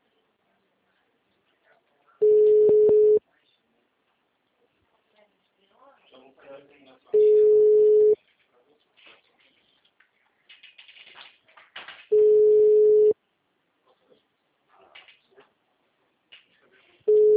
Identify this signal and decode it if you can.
Ringback tone (Europe, 425 Hz), with speech.